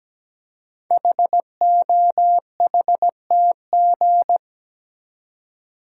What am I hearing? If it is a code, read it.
Morse code, heard as a keyed tone, about 17 wpm, 700 Hz. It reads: HOHTG
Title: Code Groups - 5 Letters - 17wpm